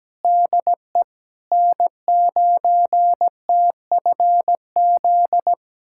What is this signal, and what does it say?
Morse code, heard as a keyed tone, about 17 wpm, 700 Hz. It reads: DE N9TFZ
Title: USA Callsign DE Callsign - 5 Characters - 17wpm